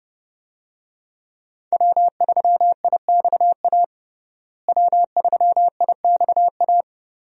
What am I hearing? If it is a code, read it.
Morse code, heard as a keyed tone, about 28 wpm, 700 Hz. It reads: W3IXA W3IXA
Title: USA Call Signs - 5 Characters - Repeated 3x - 30wpm